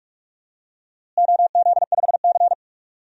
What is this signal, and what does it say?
Morse code, heard as a keyed tone, about 46 wpm, 700 Hz. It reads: OZ5C
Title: DXCC-100 Call Signs - 4 Characters - Repeated 2x - 45wpm